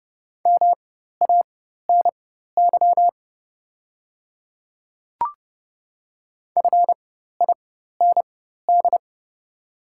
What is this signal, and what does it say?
Morse code, heard as a keyed tone, about 30 wpm, spaced as for 14 wpm, 700 Hz. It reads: MANY FIND
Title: 4x Character Spacing - Top 100 Words - 30wpm